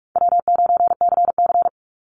Morse code, heard as a keyed tone, about 44 wpm, 700 Hz. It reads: W9CC
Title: Easy Call Signs Repeated 3x - 45wpm